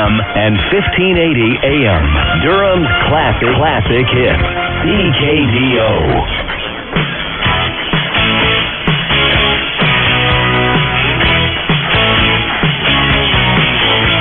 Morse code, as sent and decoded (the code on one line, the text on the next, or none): none